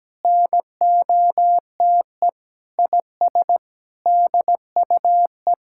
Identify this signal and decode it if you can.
Morse code, heard as a keyed tone, about 17 wpm, 700 Hz. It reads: NOTE IS DUE
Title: Sentences from Top 1000 Words - Review Part 6 - 17wpm